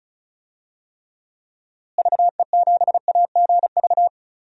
Morse code, heard as a keyed tone, about 36 wpm, 700 Hz. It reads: VE7AGV
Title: DXCC-100 Call Signs - 6 Characters - Repeated 3x - 35wpm